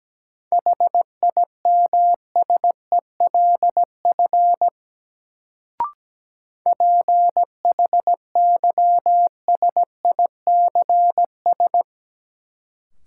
Morse code, heard as a keyed tone, about 17 wpm, 700 Hz. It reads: HIMSELF PHYSICS
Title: Sentences from Top 1500 Words - New Word Drill Alt - 17wpm